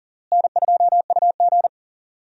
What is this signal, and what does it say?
Morse code, heard as a keyed tone, about 40 wpm, 700 Hz. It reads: N2UG
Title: USA Call Signs - 4 Characters - Repeated 3x - 40wpm